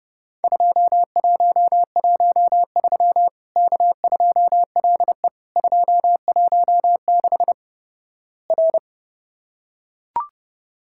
Morse code, heard as a keyed tone, about 30 wpm, 700 Hz. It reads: 2113 K2LE 216 R
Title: Worked All Europe DX Contest - QTC Exchange - 30wpm